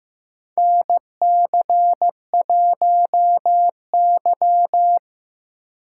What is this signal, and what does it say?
Morse code, heard as a keyed tone, about 15 wpm, 700 Hz. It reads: NC1Y
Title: USA Call Signs - 4 Characters - 2x Character Spacing - 15wpm